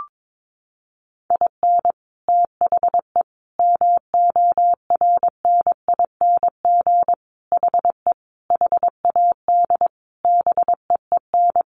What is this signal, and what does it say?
Morse code, heard as a keyed tone, about 22 wpm, 700 Hz. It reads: IN THE MORNING HE HAD BEEN
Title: Sentences from Top 1200 Words 22wpm